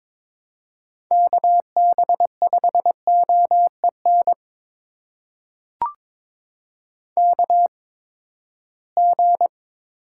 Morse code, heard as a keyed tone, about 22 wpm, spaced as for 15 wpm, 700 Hz. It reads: KB5OEN KG
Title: USA Call Signs - 8x Character Spacing - 22wpm